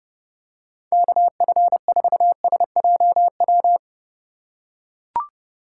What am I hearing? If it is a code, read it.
Morse code, heard as a keyed tone, about 29 wpm, 700 Hz. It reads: KF4SJW